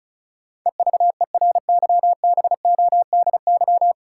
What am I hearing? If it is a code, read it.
Morse code, heard as a keyed tone, about 34 wpm, 700 Hz. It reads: EVERYBODY
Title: Sentences from Top 1400 Words - New Word Drill Alt - 35wpm